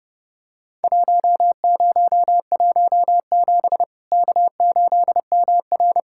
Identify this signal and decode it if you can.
Morse code, heard as a keyed tone, about 31 wpm, 700 Hz. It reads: 1017 K8MR
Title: Worked All Europe DX Contest - QTC Exchange - 30wpm